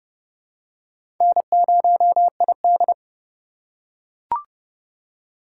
Morse code, heard as a keyed tone, about 30 wpm, 700 Hz. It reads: N0ID
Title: USA Call Signs - 4 Characters - 8x Character Spacing - 30wpm